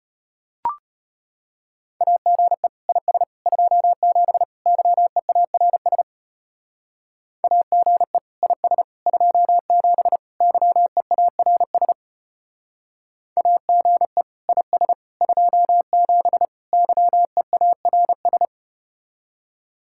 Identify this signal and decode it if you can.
Morse code, heard as a keyed tone, about 33 wpm, 700 Hz. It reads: AGE IS 27 YEARS AGE IS 27 YEARS AGE IS 27 YEARS